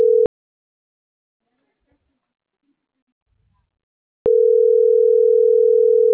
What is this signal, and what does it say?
Ringback tone (North America), 440 + 480 Hz.